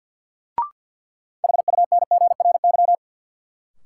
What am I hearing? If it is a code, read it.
Morse code, heard as a keyed tone, about 52 wpm, 700 Hz. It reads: HUNGRY
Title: Sentences from Top 1400 Words - New Word Drill - 50wpm